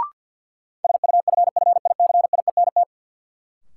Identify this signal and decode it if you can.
Morse code, heard as a keyed tone, about 52 wpm, 700 Hz. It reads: SUFFICIENT
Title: Sentences from Top 1400 Words - New Word Drill - 50wpm